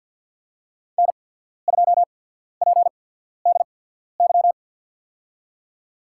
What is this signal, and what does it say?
Morse code, heard as a keyed tone, about 51 wpm, spaced as for 14 wpm, 700 Hz. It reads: N2PDY
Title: USA Call Signs - 5 Characters - 8x Character Spacing - 50wpm